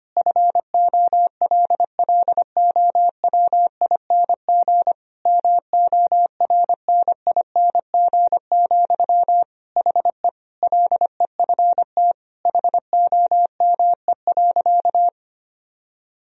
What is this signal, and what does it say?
Morse code, heard as a keyed tone, about 25 wpm, 700 Hz. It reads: FOLLOWING MORNING, HE LEFT HOME.
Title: Sentences from Top 1000 Words - Review Part 4 - 25wpm